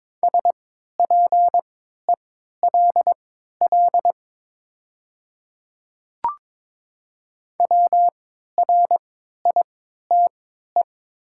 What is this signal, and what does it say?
Morse code, heard as a keyed tone, about 22 wpm, spaced as for 12 wpm, 700 Hz. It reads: SPELL WRITE
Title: 3x Character Spacing - Top 300 Words - 22wpm